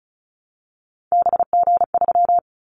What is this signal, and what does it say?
Morse code, heard as a keyed tone, about 34 wpm, 700 Hz. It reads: BG3
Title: Callsign Prefixes for Top 100 DXCC - 35wpm